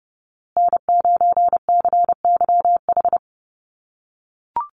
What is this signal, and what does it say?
Morse code, heard as a keyed tone, about 30 wpm, 700 Hz. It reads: N9CYH
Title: USA Call Signs Repeated 3x 30wpm